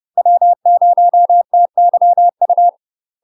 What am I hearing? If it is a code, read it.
Morse code, heard as a keyed tone, about 30 wpm, 700 Hz. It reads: W0TYU
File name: UK MASTER QSOs TRAINING / 30WPM / UK MASTER QSOs 0035